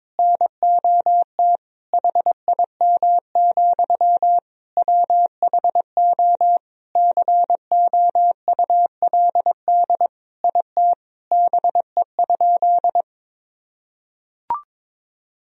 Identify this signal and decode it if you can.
Morse code, heard as a keyed tone, about 22 wpm, 700 Hz. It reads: NOT HIM, WHO COULD IT BE?
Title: Sentences from Top 1000 Words - Review Part 2 - 22wpm